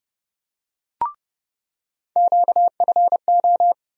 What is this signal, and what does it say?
Morse code, heard as a keyed tone, about 32 wpm, 700 Hz. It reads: QFO
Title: USA Call Sign Postfixes - 3 Letters 30wpm